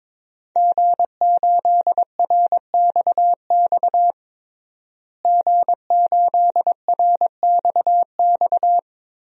Morse code, heard as a keyed tone, about 22 wpm, 700 Hz. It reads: G8RXX G8RXX
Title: DXCC-100 Call Signs - 5 Characters - Repeated 2x - 22wpm